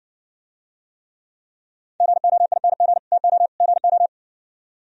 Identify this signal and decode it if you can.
Morse code, heard as a keyed tone, about 63 wpm, 700 Hz. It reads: GOING TO GO